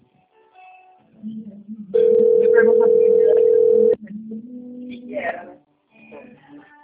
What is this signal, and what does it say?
Ringback tone (North America), 440 + 480 Hz, with speech.
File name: out- - - - - .26695